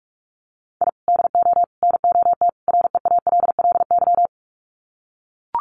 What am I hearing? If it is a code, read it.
Morse code, heard as a keyed tone, about 44 wpm, 700 Hz. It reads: I DO NOT REALLY
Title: Sets of 4 Words 45wpm